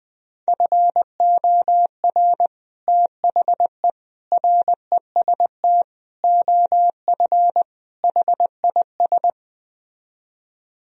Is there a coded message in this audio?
Yes: Morse code from 0.5 to 9.3 s, heard as a keyed tone, about 20 wpm, 700 Hz. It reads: FOR THE REST OF HIS